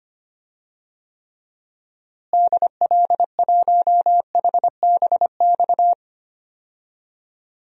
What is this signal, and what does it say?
Morse code, heard as a keyed tone, about 25 wpm, 700 Hz. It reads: DL1HBX
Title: DXCC-100 Call Signs - 6 Characters - Repeated 2x - 25wpm